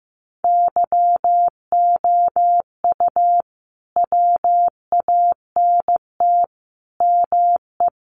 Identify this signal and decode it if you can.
Morse code, heard as a keyed tone, about 15 wpm, 700 Hz. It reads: YOU WANT ME
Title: Sets of 4 Words 15wpm